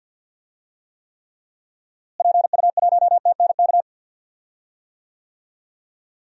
Morse code, heard as a keyed tone, about 46 wpm, 700 Hz. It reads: PU1TNK
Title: DXCC-100 Call Signs - 6 Characters - Repeated 3x - 50wpm